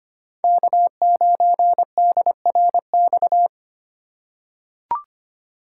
Morse code, heard as a keyed tone, about 25 wpm, 700 Hz. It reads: K9DRX
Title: USA Call Signs - 2x Character Spacing - 25wpm